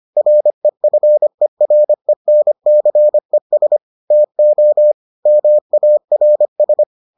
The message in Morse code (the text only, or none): REFERENCES TO MARS